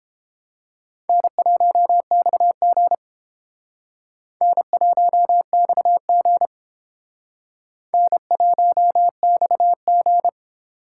Morse code, heard as a keyed tone, about 29 wpm, 700 Hz. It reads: N1XG N1XG N1XG